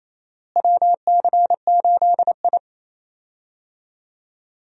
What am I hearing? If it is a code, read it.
Morse code, heard as a keyed tone, about 28 wpm, 700 Hz. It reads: WC8I
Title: USA Call Signs - 4 Characters - Repeated 2x - 28wpm